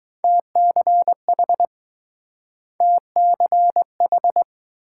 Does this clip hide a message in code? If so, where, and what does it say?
Yes: Morse code from 0.2 to 4.4 s, heard as a keyed tone, about 20 wpm, 700 Hz. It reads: TCH TCH